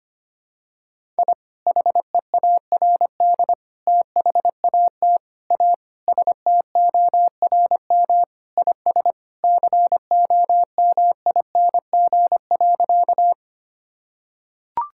Morse code, heard as a keyed tone, about 25 wpm, 700 Hz. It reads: I HEARD THAT A STORM IS COMING.
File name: practice / 25wpm / Sentences with 2-Word Phrases - 25wpm